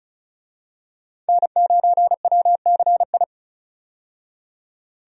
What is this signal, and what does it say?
Morse code, heard as a keyed tone, about 36 wpm, 700 Hz. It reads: N9WCI
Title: USA Call Signs - 5 Characters - 35wpm